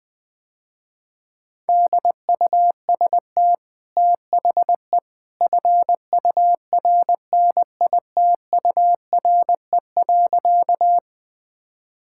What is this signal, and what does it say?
Morse code, heard as a keyed tone, about 20 wpm, 700 Hz. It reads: DUST THE FURNITURE.
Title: Easy Sentences with Hints - 20wpm